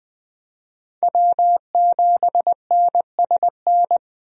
Morse code, heard as a keyed tone, about 20 wpm, 700 Hz. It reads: W7NSN
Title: USA Call Signs - 5 Characters - 20wpm